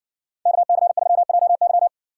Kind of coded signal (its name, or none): Morse code